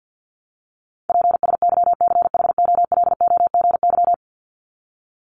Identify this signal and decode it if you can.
Morse code, heard as a keyed tone, about 50 wpm, 700 Hz. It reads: PSYCHOLOGY
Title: Top 1251-1500 Words 50wpm